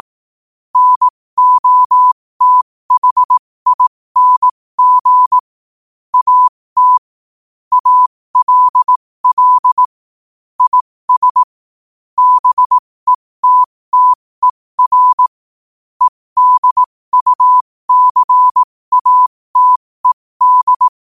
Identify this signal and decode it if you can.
Morse code, heard as a keyed tone, about 18 wpm, spaced as for 15 wpm, 1000 Hz. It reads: NOTHING AT ALL IS BETTER EDUCATED
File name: qotdfiles / 15wpm